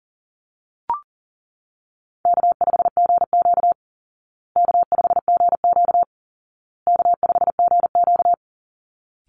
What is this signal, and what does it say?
Morse code, heard as a keyed tone, about 40 wpm, 700 Hz. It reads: K5GQ K5GQ K5GQ
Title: Easy Call Signs Repeated 3x - 40wpm